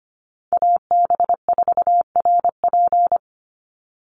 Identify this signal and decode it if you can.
Morse code, heard as a keyed tone, about 25 wpm, 700 Hz. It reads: AB4RP
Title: USA Call Signs 25wpm